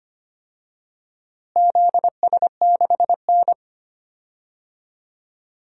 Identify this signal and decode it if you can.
Morse code, heard as a keyed tone, about 25 wpm, 700 Hz. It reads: ZS6N
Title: DXCC-100 Call Signs - 4 Characters - Repeated 3x - 25wpm